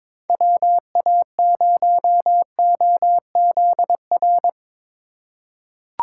Morse code, heard as a keyed tone, about 23 wpm, 700 Hz. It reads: WA0OZR
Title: USA Call Signs - 6 Characters - 3x Character Spacing - 22wpm